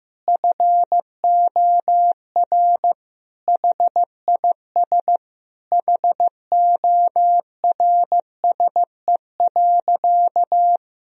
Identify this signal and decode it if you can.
Morse code, heard as a keyed tone, about 15 wpm, 700 Hz. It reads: FOR HIS HORSE.
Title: Sentences from Top 1000 Words - Review Part 3 - 15wpm